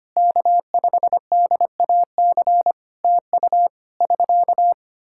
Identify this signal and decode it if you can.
Morse code, heard as a keyed tone, about 25 wpm, 700 Hz. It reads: K5DAC TU <SK>
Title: K1USN SST QSO Practice 25wpm